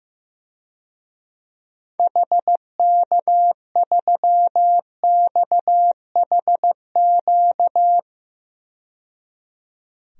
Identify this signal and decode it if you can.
Morse code, heard as a keyed tone, about 15 wpm, 700 Hz. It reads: HK3XHQ
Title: DXCC-100 Call Signs - 6 Characters - 15wpm